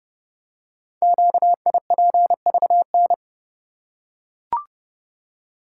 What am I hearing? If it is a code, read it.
Morse code, heard as a keyed tone, about 30 wpm, 700 Hz. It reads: QIPVN
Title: Code Groups - 5 Letter:Numbers - 30wpm